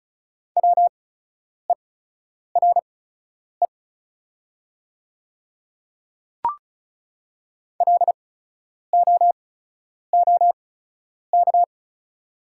Morse code, heard as a keyed tone, about 36 wpm, spaced as for 10 wpm, 700 Hz. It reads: WERE LOOK
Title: 8x Character Spacing - Top 100 Words - 35wpm